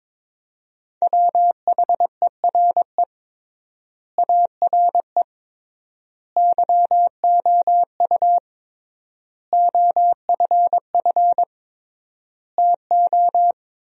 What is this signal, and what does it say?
Morse code, heard as a keyed tone, about 22 wpm, 700 Hz. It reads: WHERE ARE YOU OFF TO
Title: 3x Word-Spacing - Sentences from Top 400 Words - 22wpm